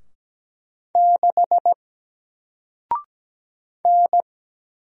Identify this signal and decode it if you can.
Morse code, heard as a keyed tone, about 17 wpm, 700 Hz. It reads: 6 N
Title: Single Character (Letter:Number:Punctuation:Prosign) 17wpm